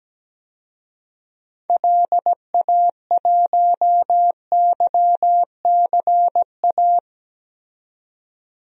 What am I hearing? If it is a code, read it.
Morse code, heard as a keyed tone, about 17 wpm, 700 Hz. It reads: LA1YCA
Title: DXCC-100 Call Signs - 6 Characters - 17wpm